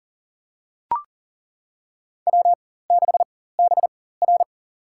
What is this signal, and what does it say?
Morse code, heard as a keyed tone, about 40 wpm, spaced as for 19 wpm, 700 Hz. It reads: W6BR
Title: USA Call Signs - 4 Characters - 4x Character Spacing - 40wpm